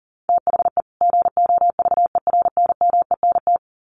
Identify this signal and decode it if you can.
Morse code, heard as a keyed tone, about 40 wpm, 700 Hz. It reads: THE GOVERNMENT
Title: Sets of 2 Words 40wpm